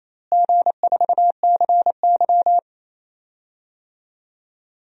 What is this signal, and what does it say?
Morse code, heard as a keyed tone, about 28 wpm, 700 Hz. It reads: G4CY
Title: DXCC-100 Call Signs - 4 Characters - Repeated 2x - 28wpm